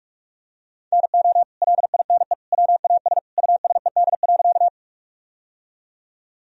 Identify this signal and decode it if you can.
Morse code, heard as a keyed tone, about 47 wpm, 700 Hz. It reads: NO LINE WAS USED.
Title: Sentences from Top 1000 Words - Review Part 1 - 45wpm